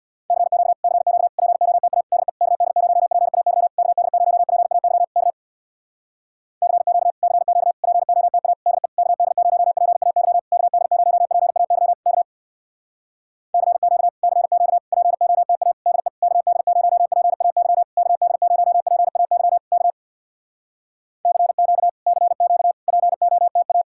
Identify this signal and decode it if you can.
Morse code, heard as a keyed tone, about 66 wpm, 700 Hz. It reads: CQ CQ POTA DE KD0PAY KD0PAY K CQ CQ POTA DE KD0PAY KD0PAY K CQ CQ POTA DE KD0PAY KD0PAY K CQ CQ POTA